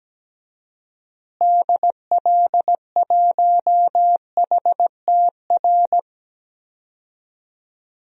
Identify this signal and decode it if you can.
Morse code, heard as a keyed tone, about 17 wpm, 700 Hz. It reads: DL1HTR
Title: DXCC-100 Call Signs - 6 Characters - Repeated 2x - 17wpm